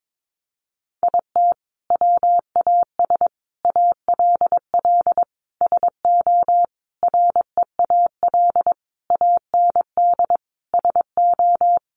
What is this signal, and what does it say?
Morse code, heard as a keyed tone, about 22 wpm, 700 Hz. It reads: IT WAS ALL SO REAL AND SO
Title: Sentences from Top 1000 Words 22wpm